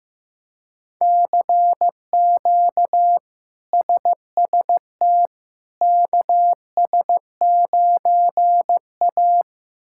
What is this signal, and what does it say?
Morse code, heard as a keyed tone, about 15 wpm, 700 Hz. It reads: CQ SST KS9A
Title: K1USN SST QSO Practice - 15wpm